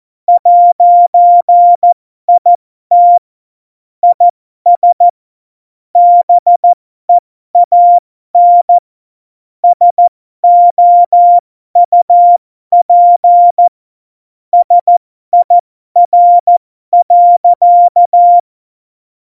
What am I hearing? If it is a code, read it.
Morse code, heard as a keyed tone, about 14 wpm, spaced as for 12 wpm, 700 Hz. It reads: 'IT IS BEAN SOUP SIR.
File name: Morsekurs / Witze